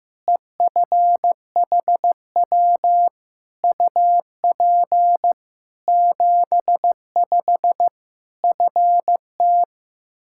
Morse code, heard as a keyed tone, about 15 wpm, 700 Hz. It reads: EFHW UP 75 FT
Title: K1USN SST QSO Practice 15wpm